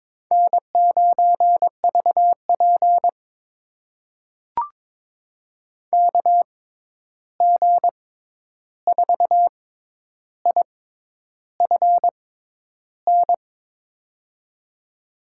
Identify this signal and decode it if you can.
Morse code, heard as a keyed tone, about 22 wpm, spaced as for 10 wpm, 700 Hz. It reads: N9VP KG4IFN
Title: USA Call Signs - 6x Character Spacing - 22wpm